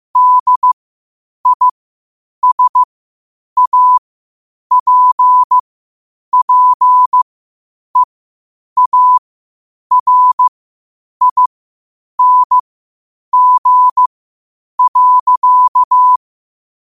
Morse code, heard as a keyed tone, about 15 wpm, spaced as for 8 wpm, 1000 Hz. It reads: DISAPPEARING.
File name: qotdfiles / 8wpm